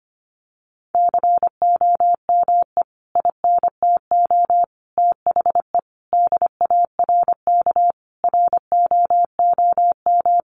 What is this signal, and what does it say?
Morse code, heard as a keyed tone, about 25 wpm, 700 Hz. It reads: COME INTO THE DARK ROOM
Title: Sentences from Top 1000 Words 25wpm